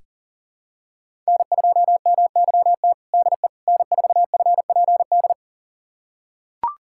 Morse code, heard as a keyed tone, about 40 wpm, 700 Hz. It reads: N2MYT DE N4FPD
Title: USA Callsign DE Callsign - 5 Characters - 40wpm